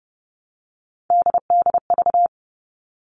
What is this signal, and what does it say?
Morse code, heard as a keyed tone, about 30 wpm, 700 Hz. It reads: DDV
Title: All Cut Numbers - 3-Digit 30wpm